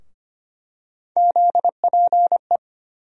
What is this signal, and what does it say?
Morse code, heard as a keyed tone, about 25 wpm, 700 Hz. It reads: ZPE
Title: USA Call Sign Postfixes - 3 Letters 25wpm